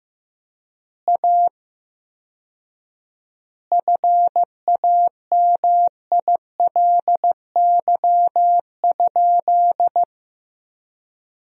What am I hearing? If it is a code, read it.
Morse code, heard as a keyed tone, about 16 wpm, 700 Hz. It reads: A FAMILY?
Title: 4x Word-Spacing - Sentences from Top 400 Words - 15wpm